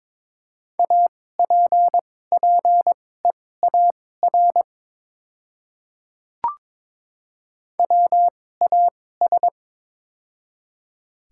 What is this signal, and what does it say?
Morse code, heard as a keyed tone, about 22 wpm, spaced as for 16 wpm, 700 Hz. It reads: APPEAR WAS
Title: 2x Character Spacing - Top 500 Words - 22wpm